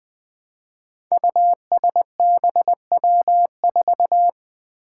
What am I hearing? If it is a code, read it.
Morse code, heard as a keyed tone, about 20 wpm, 700 Hz. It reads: USBW4
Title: Five Challenging Letter-Number Combinations 20wpm